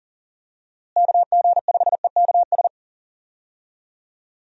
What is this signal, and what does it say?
Morse code, heard as a keyed tone, about 39 wpm, 700 Hz. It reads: KG5EKS
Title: USA Call Signs - 6 Characters - Repeated 2x - 40wpm